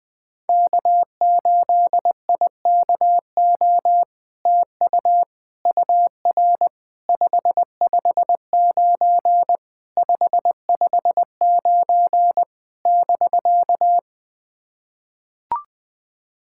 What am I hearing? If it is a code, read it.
Morse code, heard as a keyed tone, about 20 wpm, 700 Hz. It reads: K8IKO TU UR 559 559 <BK>
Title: POTA QSO Practice - 20wpm